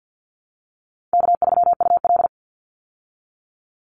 Morse code, heard as a keyed tone, about 50 wpm, 700 Hz. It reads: K3UL
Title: Easy Call Signs Repeated 3x - 50wpm